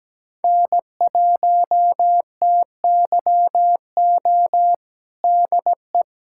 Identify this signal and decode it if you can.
Morse code, heard as a keyed tone, about 17 wpm, 700 Hz. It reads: N1TYO DE
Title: USA Callsign DE Callsign - 5 Characters - 17wpm